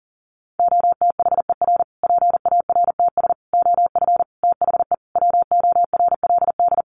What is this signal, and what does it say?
Morse code, heard as a keyed tone, about 40 wpm, 700 Hz. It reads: OTHER PARTS OF THE WORLD
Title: Sets of 5 Words Encore 40wpm